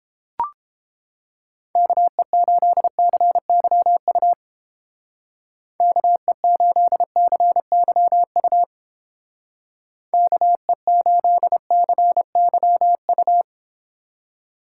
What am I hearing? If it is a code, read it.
Morse code, heard as a keyed tone, about 30 wpm, 700 Hz. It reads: KE8CYU KE8CYU KE8CYU